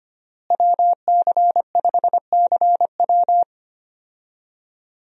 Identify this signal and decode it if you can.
Morse code, heard as a keyed tone, about 25 wpm, 700 Hz. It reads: WC5CW
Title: USA Call Signs - 5 Characters - Repeated 3x - 25wpm